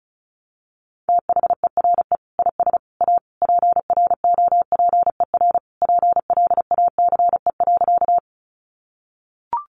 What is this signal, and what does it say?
Morse code, heard as a keyed tone, about 34 wpm, 700 Hz. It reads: THERE IS A PROPER PLACE.
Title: Sentences from Top 800 Words 35wpm